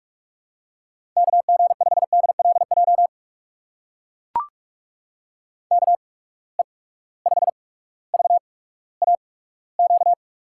Morse code, heard as a keyed tone, about 44 wpm, spaced as for 19 wpm, 700 Hz. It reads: KG5DLJ KE5VAQ
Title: USA Call Signs - 6 Characters - 8x Character Spacing - 45wpm